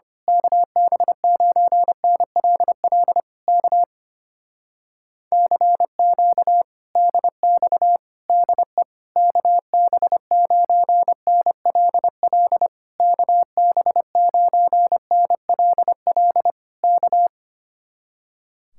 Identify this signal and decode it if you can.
Morse code, heard as a keyed tone, about 24 wpm, 700 Hz. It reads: KB9NLL K CQ DX DE KB9NLL KB9NLL K
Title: Speed-Racing Common QSO Phrases Encore - 25wpm